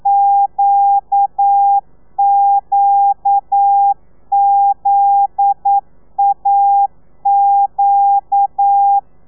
Morse code, heard as a keyed tone, about 9 wpm, 800 Hz. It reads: QQZAQ
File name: telegrafia / i2tfj / Lez9 40